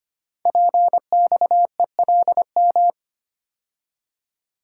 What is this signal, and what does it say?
Morse code, heard as a keyed tone, about 25 wpm, 700 Hz. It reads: PXELM
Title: Code Groups - 5 Letters - 25wpm